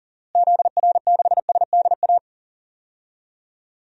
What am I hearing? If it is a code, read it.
Morse code, heard as a keyed tone, about 40 wpm, 700 Hz. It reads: ZR6SDA
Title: DXCC-100 Call Signs - 6 Characters - Repeated 3x - 40wpm